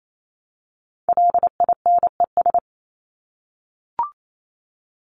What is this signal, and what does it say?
Morse code, heard as a keyed tone, about 28 wpm, 700 Hz. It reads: LINES
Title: ES:S Suffix - Top 100 Words with Base Words 28wpm